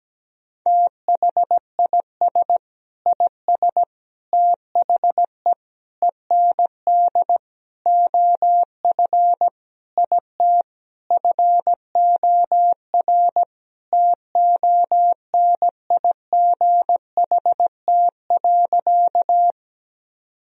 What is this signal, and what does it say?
Morse code, heard as a keyed tone, about 17 wpm, 700 Hz. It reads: THIS IS THE END OF IT FOR TONIGHT.
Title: Sentences from Top 1000 Words - Review Part 3 - 17wpm